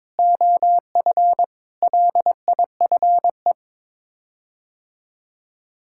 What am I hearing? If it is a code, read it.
Morse code, heard as a keyed tone, about 21 wpm, 700 Hz. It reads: OF LIFE
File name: practice / 22wpm / Sets of 8 Words - 22wpm